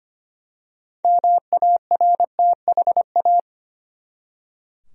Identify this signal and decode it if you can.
Morse code, heard as a keyed tone, about 25 wpm, 700 Hz. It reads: MARTHA
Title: USA Common Names - 25wpm